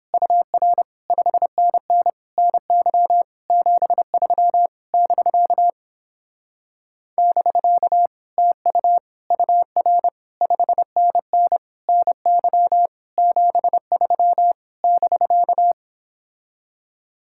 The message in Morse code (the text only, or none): UR 5NN NY 73 <BK> <BK> TU UR 5NN NY 73 <BK>